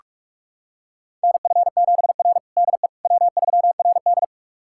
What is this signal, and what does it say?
Morse code, heard as a keyed tone, about 42 wpm, 690 Hz. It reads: NF7R DE W3RD
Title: USA Callsign DE Callsign - 4 Characters - 45wpm